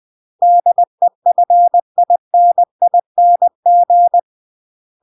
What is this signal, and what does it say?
Morse code, heard as a keyed tone, about 20 wpm, 700 Hz. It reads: DEFINING